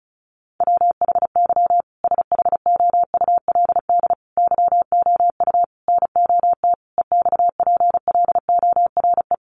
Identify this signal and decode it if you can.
Morse code, heard as a keyed tone, about 34 wpm, 700 Hz. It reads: WHY SHOULD YOU NOT EXPLORE
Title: Sentences from Top 1100 Words 35wpm